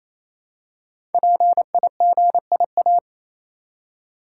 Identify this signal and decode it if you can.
Morse code, heard as a keyed tone, about 27 wpm, 700 Hz. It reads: PIGIA
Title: Code Groups - 5 Letters - 28wpm